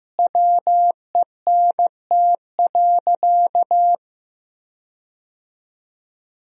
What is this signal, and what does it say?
Morse code, heard as a keyed tone, about 16 wpm, 700 Hz. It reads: WENT.
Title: 3x Word-Spacing - Sentences from Top 200 Words - 15wpm